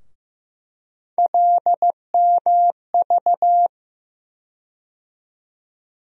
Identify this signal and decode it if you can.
Morse code, heard as a keyed tone, about 15 wpm, 700 Hz. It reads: LMV
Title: USA Call Sign Postfixes - 3 Letters 15wpm